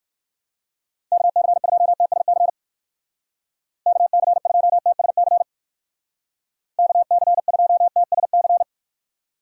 Morse code, heard as a keyed tone, about 54 wpm, 700 Hz. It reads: KC2TSC KC2TSC KC2TSC
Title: Speed Racing - USA Call Signs 45wpm